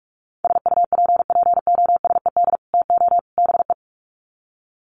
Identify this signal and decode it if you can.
Morse code, heard as a keyed tone, about 44 wpm, 700 Hz. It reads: SUPPOSED TO BE
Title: Sets of 3 Words 45wpm